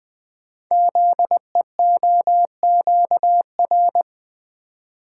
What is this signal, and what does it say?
Morse code, heard as a keyed tone, about 20 wpm, 700 Hz. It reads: ZEOQR
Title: Code Groups - 5 Letters - 20wpm